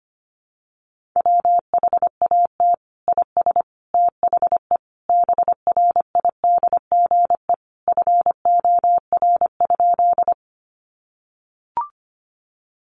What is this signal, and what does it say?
Morse code, heard as a keyed tone, about 25 wpm, 700 Hz. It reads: WHAT IS THE BRIDGE FOR?
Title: Sentences from Top 1200 Words 25wpm